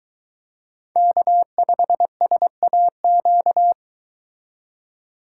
Morse code, heard as a keyed tone, about 23 wpm, 700 Hz. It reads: K5SAQ